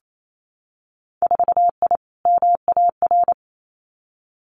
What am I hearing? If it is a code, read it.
Morse code, heard as a keyed tone, about 28 wpm, 700 Hz. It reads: 4I MAR
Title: Winter Field Day Exchanges Encore 28wpm